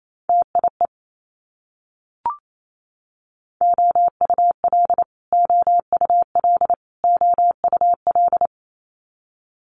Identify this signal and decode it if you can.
Morse code, heard as a keyed tone, about 28 wpm, 700 Hz. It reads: TIE OUL OUL OUL
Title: Multi-letter Phonemes Repeated 3x 28wpm